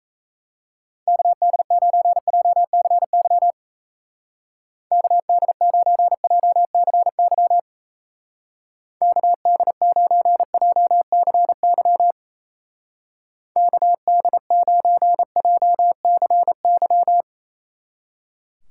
Morse code, heard as a keyed tone, about 35 wpm, 700 Hz. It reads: KD9JCY KD9JCY KD9JCY KD9JCY
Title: Speed Racing - USA 2x3 Call Signs 28wpm